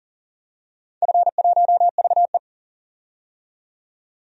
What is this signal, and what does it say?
Morse code, heard as a keyed tone, about 40 wpm, 700 Hz. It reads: F1VE
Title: DXCC-100 Call Signs - 4 Characters - 40wpm